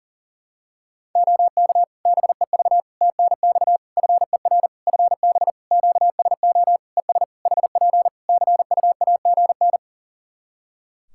Morse code, heard as a keyed tone, about 40 wpm, 700 Hz. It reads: OK BEV TNX FER FB QSO ES HP CUAGN
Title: Practice QSO - Exchange 4 - Ending - 40wpm